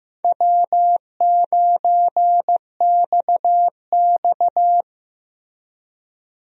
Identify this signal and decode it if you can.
Morse code, heard as a keyed tone, about 15 wpm, 700 Hz. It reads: W9XX
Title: USA Call Signs - 4 Characters - Repeated 3x - 15wpm